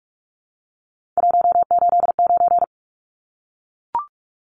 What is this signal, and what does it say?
Morse code, heard as a keyed tone, about 44 wpm, 700 Hz. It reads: 189
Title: Numbers - 3-Digit 45wpm